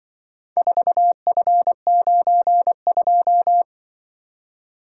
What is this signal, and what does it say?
Morse code, heard as a keyed tone, about 24 wpm, 700 Hz. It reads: 4F92
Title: Speed-Racing - Four Letter-Number Combinations - 20wpm